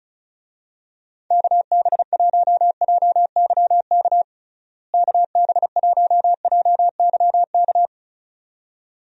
Morse code, heard as a keyed tone, about 36 wpm, 700 Hz. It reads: KB1JYK KB1JYK
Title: USA Call Signs - 6 Characters - Repeated 2x - 35wpm